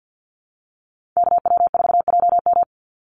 Morse code, heard as a keyed tone, about 50 wpm, 700 Hz. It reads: KW4JM
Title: NA SOTA Chaser Callsigns 50wpm